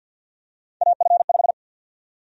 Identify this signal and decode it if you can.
Morse code, heard as a keyed tone, about 52 wpm, 700 Hz. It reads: AF5
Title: USA Call Sign Prefixes - 50wpm